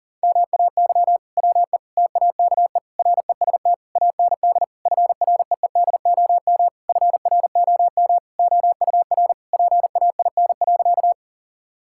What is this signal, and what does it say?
Morse code, heard as a keyed tone, about 40 wpm, 700 Hz. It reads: MAY WE TAKE REST AND FREEDOM FROM OUR PAIN.
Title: Sentences from Top 1600 Words 40wpm